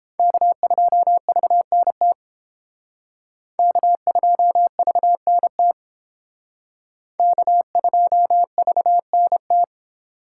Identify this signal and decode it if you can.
Morse code, heard as a keyed tone, about 29 wpm, 700 Hz. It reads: K2VNT K2VNT K2VNT